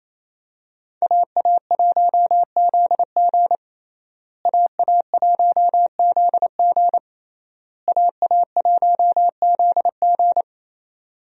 Morse code, heard as a keyed tone, about 28 wpm, 700 Hz. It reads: AA1ZG AA1ZG AA1ZG
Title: USA Call Signs - 5 Characters - Repeated 3x - 28wpm